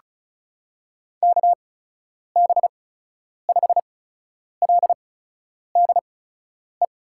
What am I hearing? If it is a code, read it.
Morse code, heard as a keyed tone, about 34 wpm, spaced as for 10 wpm, 700 Hz. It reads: KB5LDE
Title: USA Call Signs - 6 Characters - 8x Character Spacing - 35wpm